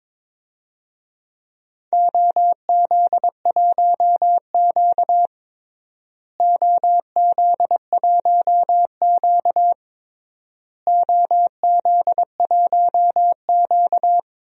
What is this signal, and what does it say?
Morse code, heard as a keyed tone, about 22 wpm, 700 Hz. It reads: OZ1Q OZ1Q OZ1Q
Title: DXCC-100 Call Signs - 4 Characters - Repeated 3x - 22wpm